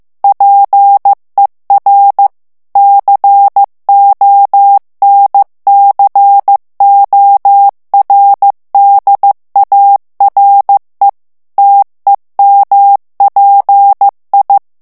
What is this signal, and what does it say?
Morse code, heard as a keyed tone, about 15 wpm, 800 Hz. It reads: PER CONCORDARE TEMPI
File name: ITALIEN texte clair 1 vitesse 14 mots minute